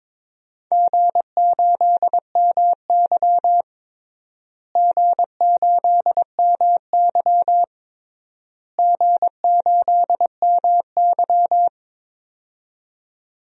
Morse code, heard as a keyed tone, about 22 wpm, 700 Hz. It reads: G8MY G8MY G8MY
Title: DXCC-100 Call Signs - 4 Characters - Repeated 3x - 22wpm